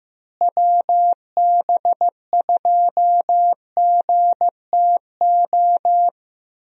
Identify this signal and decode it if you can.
Morse code, heard as a keyed tone, about 15 wpm, 700 Hz. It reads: WB2GTO
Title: USA Call Signs - 6 Characters - Repeated 3x - 15wpm